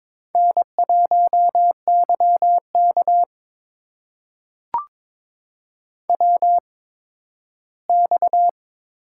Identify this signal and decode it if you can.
Morse code, heard as a keyed tone, about 22 wpm, spaced as for 13 wpm, 700 Hz. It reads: N1YK WX
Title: USA Call Signs - 4 Characters - 8x Character Spacing - 22wpm